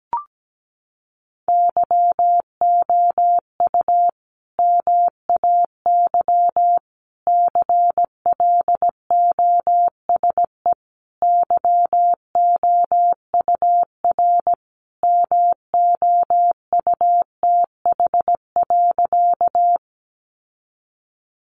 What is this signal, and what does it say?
Morse code, heard as a keyed tone, about 17 wpm, 700 Hz. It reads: YOU MAY CLOSE YOUR MOUTH.
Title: Sentences from Top 1500 Words 17wpm